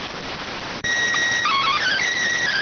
SSTV (slow-scan television), PD 290.